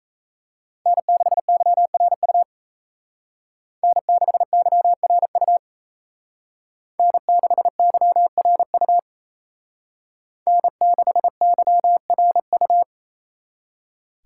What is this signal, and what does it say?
Morse code, heard as a keyed tone, about 33 wpm, 700 Hz. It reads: N6YRU N6YRU N6YRU N6YRU
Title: Speed Racing - USA 1x3 Call Signs 28wpm